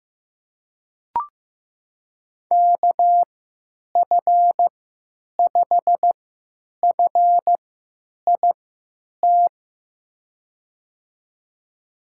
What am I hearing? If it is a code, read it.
Morse code, heard as a keyed tone, about 15 wpm, spaced as for 9 wpm, 700 Hz. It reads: KF5FIT
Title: USA Call Signs - 6 Characters - 3x Character Spacing - 15wpm